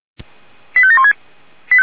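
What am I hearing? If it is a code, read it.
Five-tone selective call (ZVEI), heard as quick runs of five tones, tones of about 70 ms. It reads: 85716
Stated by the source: Signal received via SDR, ZVEI five-tone.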